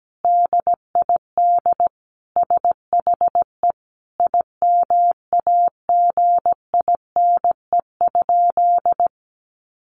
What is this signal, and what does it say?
Morse code, heard as a keyed tone, about 17 wpm, 700 Hz. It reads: DID SHE IMAGINE?
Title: Sentences from Top 1400 Words 17wpm